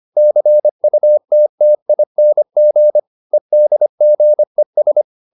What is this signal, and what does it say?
Morse code, heard as a keyed tone, about 25 wpm, 600 Hz. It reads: CUTTING EDGES